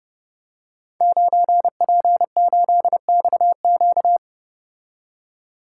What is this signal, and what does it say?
Morse code, heard as a keyed tone, about 30 wpm, 700 Hz. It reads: 9P8XQ
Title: Code Groups - 5 Letter:Numbers - 30wpm